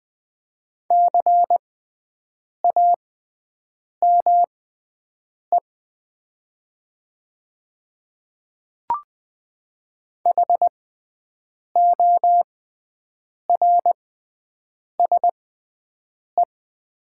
Morse code, heard as a keyed tone, about 20 wpm, spaced as for 7 wpm, 700 Hz. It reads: CAME HORSE
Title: 6x Character Spacing - Top 500 Words - 20wpm